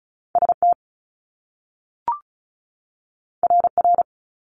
Morse code, heard as a keyed tone, about 36 wpm, 700 Hz. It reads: ST RR
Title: Top Two Letter Combinations in Words 35wpm